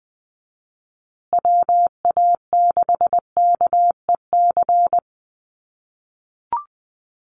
Morse code, heard as a keyed tone, about 20 wpm, 700 Hz. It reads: WA6KEC
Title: USA Call Signs Repeated 2x 20wpm